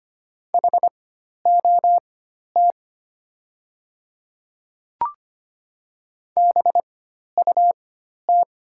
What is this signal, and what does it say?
Morse code, heard as a keyed tone, about 25 wpm, spaced as for 12 wpm, 700 Hz. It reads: HOT BUT
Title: 4x Character Spacing - Top 100 Words - 25wpm